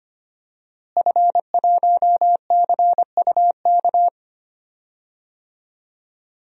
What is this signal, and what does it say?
Morse code, heard as a keyed tone, about 25 wpm, 700 Hz. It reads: F1CUK
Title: DXCC-100 Call Signs - 5 Characters - Repeated 3x - 25wpm